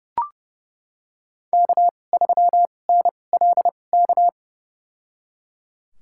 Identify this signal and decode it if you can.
Morse code, heard as a keyed tone, about 30 wpm, spaced as for 22 wpm, 700 Hz. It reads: K3NLK
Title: USA Call Signs - 5 Characters - 2x Character Spacing - 30wpm